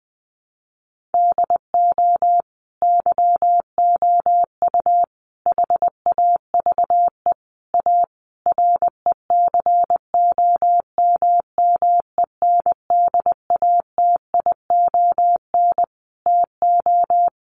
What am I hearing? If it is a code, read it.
Morse code, heard as a keyed tone, about 20 wpm, 700 Hz. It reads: DO YOU HAVE A RECOMMENDATION TO
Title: Sentences from Top 1400 Words 20wpm